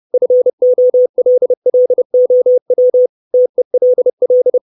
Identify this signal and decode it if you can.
Morse code, heard as a keyed tone, about 30 wpm, 500 Hz. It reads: FOLLOW TELL
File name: 632 Words 30WPM